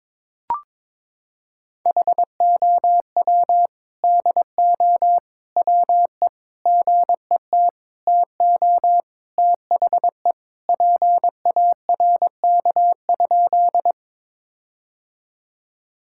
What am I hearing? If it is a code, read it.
Morse code, heard as a keyed tone, about 22 wpm, 700 Hz. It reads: HOW DO WE GET TO THE PARK?
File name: practice / 22wpm / Sentences with 4-Word Phrases - 22wpm